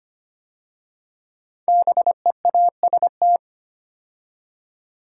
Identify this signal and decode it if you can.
Morse code, heard as a keyed tone, about 25 wpm, 700 Hz. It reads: BEAST
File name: 5 Letter Words - First 3 Letters Given - 25wpm